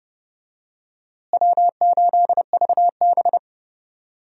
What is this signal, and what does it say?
Morse code, heard as a keyed tone, about 30 wpm, 700 Hz. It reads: W8VB
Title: USA Call Signs - 4 Characters - Repeated 3x - 30wpm